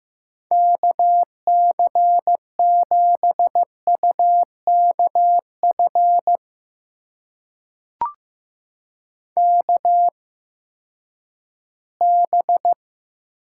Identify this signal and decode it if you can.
Morse code, heard as a keyed tone, about 15 wpm, spaced as for 10 wpm, 700 Hz. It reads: KC7UKF KB
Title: USA Call Signs - 6 Characters - 8x Character Spacing - 15wpm